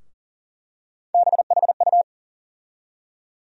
Morse code, heard as a keyed tone, about 40 wpm, 700 Hz. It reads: BHU
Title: USA Call Sign Postfixes - 3 Letters 40wpm